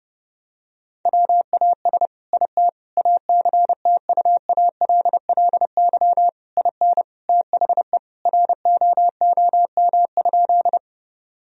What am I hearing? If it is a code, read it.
Morse code, heard as a keyed tone, about 31 wpm, 700 Hz. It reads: WAS IT ACTUALLY IN THE ROOM?
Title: Sentences from Top 1000 Words - Review Part 4 - 30wpm